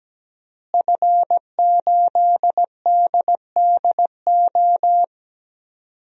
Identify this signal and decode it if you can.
Morse code, heard as a keyed tone, about 17 wpm, 700 Hz. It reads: F8DDO
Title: DXCC-100 Call Signs - 5 Characters - Repeated 3x - 17wpm